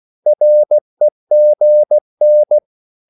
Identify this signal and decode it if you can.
Morse code, heard as a keyed tone, about 16 wpm, 600 Hz. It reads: REGN
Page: Korte danske ord 16wpm